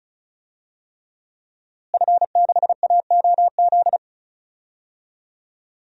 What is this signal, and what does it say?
Morse code, heard as a keyed tone, about 34 wpm, 700 Hz. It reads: F6AOZ
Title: DXCC-100 Call Signs - 5 Characters - Repeated 3x - 35wpm